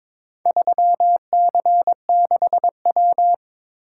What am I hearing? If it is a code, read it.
Morse code, heard as a keyed tone, about 22 wpm, 700 Hz. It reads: 3C6W
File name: practice / 22wpm / Four Letter-Number Combinations - 22wpm